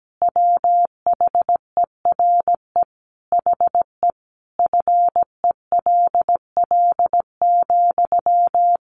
Morse code, heard as a keyed tone, about 17 wpm, 700 Hz. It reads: WHERE HE FELL,
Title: A Sound Of Thunder 17wpm